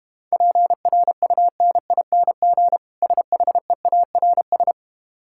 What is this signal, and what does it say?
Morse code, heard as a keyed tone, about 32 wpm, 690 Hz. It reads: PRUNING SHEARS